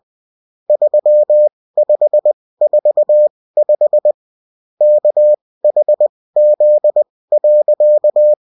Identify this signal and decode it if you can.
Morse code, heard as a keyed tone, about 20 wpm, spaced as for 16 wpm, 600 Hz. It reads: 3545 KHZ.